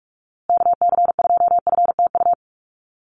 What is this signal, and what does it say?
Morse code, heard as a keyed tone, about 44 wpm, 700 Hz. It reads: KC2FTU
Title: USA Call Signs Repeated 2x 45wpm